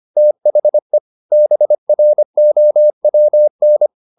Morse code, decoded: THE BROWN

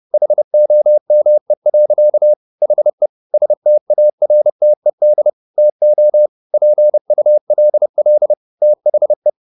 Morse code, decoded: HOME. HE STARTED TO PULL THE

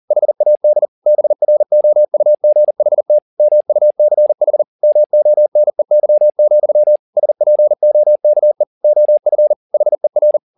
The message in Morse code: HAD BROUGHT MUCH MONEY, SPOKE OF HER

